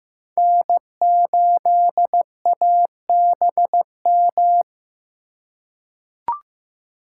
Morse code, heard as a keyed tone, about 16 wpm, 700 Hz. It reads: N8ABM
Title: USA Call Signs - 5 Characters - 3x Character Spacing - 15wpm